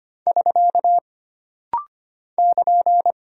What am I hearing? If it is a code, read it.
Morse code, heard as a keyed tone, about 25 wpm, 700 Hz. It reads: <SK> <KN>
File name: Single Punctuation-Prosign - Rapid-Fire - 25wpm